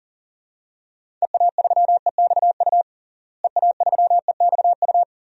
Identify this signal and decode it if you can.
Morse code, heard as a keyed tone, about 40 wpm, 700 Hz. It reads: EA3EXU EA3EXU
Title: DXCC-100 Call Signs - 6 Characters - Repeated 2x - 40wpm